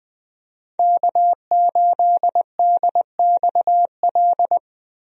Morse code, heard as a keyed tone, about 20 wpm, 700 Hz. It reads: K8DXL
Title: USA Call Signs - 5 Characters - Repeated 3x - 20wpm